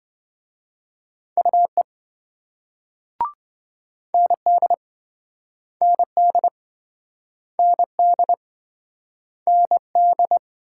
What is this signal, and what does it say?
Morse code, heard as a keyed tone, about 27 wpm, 700 Hz. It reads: UE ND ND ND ND